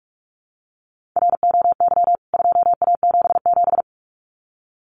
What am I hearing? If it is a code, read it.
Morse code, heard as a keyed tone, about 44 wpm, 700 Hz. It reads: ROY 2A77
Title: CWT Exchange 45wpm